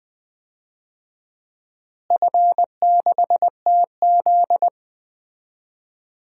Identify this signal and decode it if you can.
Morse code, heard as a keyed tone, about 20 wpm, 700 Hz. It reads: F6TZ
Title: DXCC-100 Call Signs - 4 Characters - Repeated 2x - 20wpm